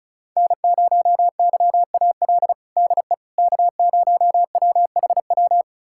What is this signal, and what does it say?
Morse code, heard as a keyed tone, about 36 wpm, 700 Hz. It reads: N0YAL DE K0WHW
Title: USA Callsign DE Callsign - 5 Characters - 35wpm